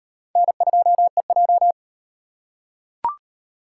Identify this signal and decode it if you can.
Morse code, heard as a keyed tone, about 35 wpm, 700 Hz. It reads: N2EJ